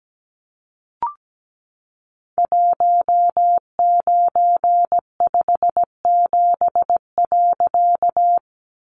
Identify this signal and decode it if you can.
Morse code, heard as a keyed tone, about 17 wpm, 700 Hz. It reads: 1957.
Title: A Sound Of Thunder 17wpm